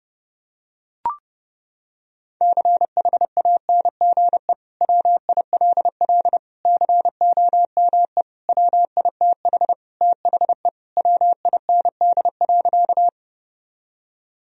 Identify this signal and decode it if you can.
Morse code, heard as a keyed tone, about 30 wpm, 700 Hz. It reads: CHANGE WILL COME WITH THE WIND.
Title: Sentences from Top 1000 Words - Review Part 2 - 30wpm